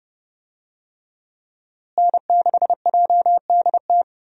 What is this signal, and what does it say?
Morse code, heard as a keyed tone, about 30 wpm, 700 Hz. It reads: N6JDT
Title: USA Call Signs - 5 Characters - Repeated 3x - 30wpm